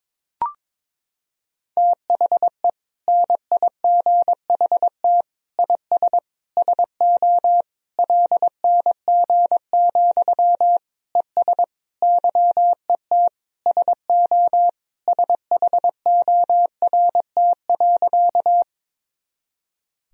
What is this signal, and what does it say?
Morse code, heard as a keyed tone, about 22 wpm, 700 Hz. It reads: THE NIGHT IS SO LNG, ES YET SO SHORT.
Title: Sentences from Top 500 Words and Abbreviations 22wpm